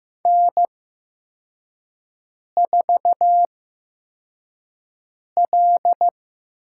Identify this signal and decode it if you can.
Morse code, heard as a keyed tone, about 15 wpm, spaced as for 4 wpm, 700 Hz. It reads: N4L
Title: USA Call Signs - 4 Characters - 8x Character Spacing - 15wpm